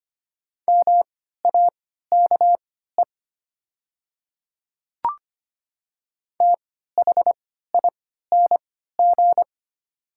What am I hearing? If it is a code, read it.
Morse code, heard as a keyed tone, about 25 wpm, spaced as for 14 wpm, 700 Hz. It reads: MAKE THING
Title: 3x Character Spacing - Top 100 Words - 25wpm